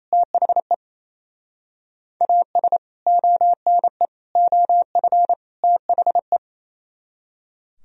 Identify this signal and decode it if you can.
Morse code, heard as a keyed tone, about 33 wpm, 700 Hz. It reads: THE AS ONE OF THE
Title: Speed-Racing - Sets of 4 Words 28wpm